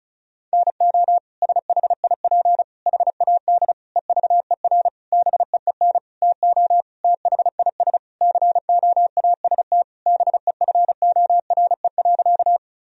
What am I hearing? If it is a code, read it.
Morse code, heard as a keyed tone, about 36 wpm, 700 Hz. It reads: NO SHIP HAD EVER BEEN TO THIS COAST BEFORE.
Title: Sentences from Top 1000 Words - Review Part 4 - 35wpm